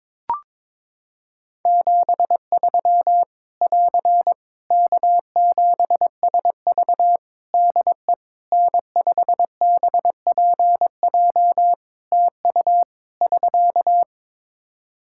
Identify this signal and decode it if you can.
Morse code, heard as a keyed tone, about 22 wpm, 700 Hz. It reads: 73 <AR> K7SV DE N5BPJ TU <SK>
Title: Practice QSO - Exchange 4 - Ending - 22wpm